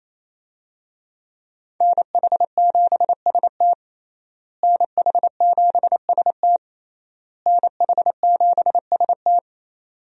Morse code, heard as a keyed tone, about 28 wpm, 700 Hz. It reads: NH7ST NH7ST NH7ST
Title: USA Call Signs - 5 Characters - Repeated 3x - 28wpm